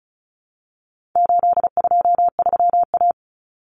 Morse code, heard as a keyed tone, about 34 wpm, 700 Hz. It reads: 823A
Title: Common Cut Numbers - 4-Digit 35wpm